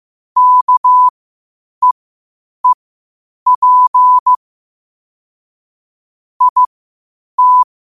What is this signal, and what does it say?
Morse code, heard as a keyed tone, about 15 wpm, spaced as for 8 wpm, 1000 Hz. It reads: KEEP IT
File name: qotdfiles / 8wpm